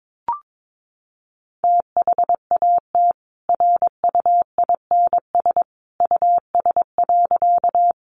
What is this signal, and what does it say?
Morse code, heard as a keyed tone, about 22 wpm, 700 Hz. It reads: THAT RUINS US.
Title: A Sound Of Thunder 22wpm